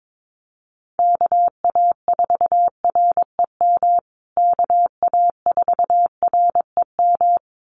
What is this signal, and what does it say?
Morse code, heard as a keyed tone, about 22 wpm, 700 Hz. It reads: KA4REM KA4REM
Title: USA Call Signs Repeated 2x 22wpm